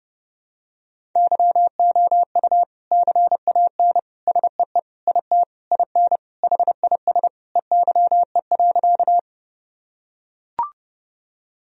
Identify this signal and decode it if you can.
Morse code, heard as a keyed tone, about 30 wpm, 700 Hz. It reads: YOU CAN SEE IT IN HIS EYE.
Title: Sentences from Top 1000 Words - Review Part 1 - 30wpm